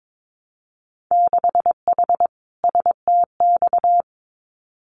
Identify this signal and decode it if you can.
Morse code, heard as a keyed tone, about 22 wpm, 700 Hz. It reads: 6H STX
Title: Winter Field Day Exchanges Encore 22wpm